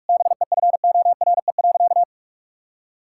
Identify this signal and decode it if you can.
Morse code, heard as a keyed tone, about 46 wpm, 700 Hz. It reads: BEFORE.
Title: Sentences from Top 1000 Words - Review Part 6 - 45wpm